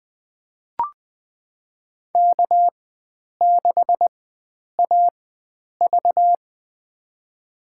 Morse code, heard as a keyed tone, about 20 wpm, spaced as for 9 wpm, 700 Hz. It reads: K6AV
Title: USA Call Signs - 4 Characters - 4x Character Spacing - 20wpm